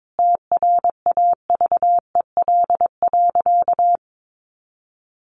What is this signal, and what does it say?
Morse code, heard as a keyed tone, about 22 wpm, 700 Hz. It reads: TRAVEL.
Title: Sentences from Top 400 Words 22wpm